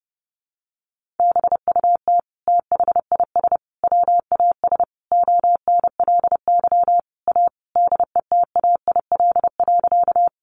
Morse code, heard as a keyed tone, about 30 wpm, 700 Hz. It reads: BUT THIS WAS ONLY A DETAIL.
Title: Sentences from Top 1500 Words 30wpm